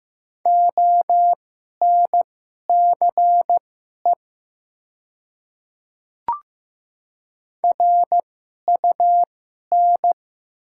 Morse code, heard as a keyed tone, about 15 wpm, spaced as for 11 wpm, 700 Hz. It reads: ONCE RUN
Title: 2x Character Spacing - Top 300 Words - 15wpm